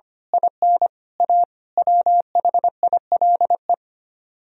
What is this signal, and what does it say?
Morse code, heard as a keyed tone, about 24 wpm, 700 Hz. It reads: IN A WHILE
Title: Speed-Racing - Sets of 5 Words 25wpm